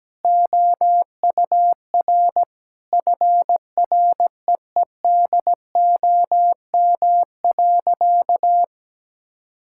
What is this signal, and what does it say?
Morse code, heard as a keyed tone, about 17 wpm, 700 Hz. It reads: OUR FREEDOM.
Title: Sentences from Top 1000 Words - Review Part 5 - 17wpm